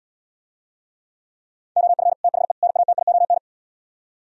Morse code, heard as a keyed tone, about 77 wpm, 700 Hz. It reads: OF THE NATION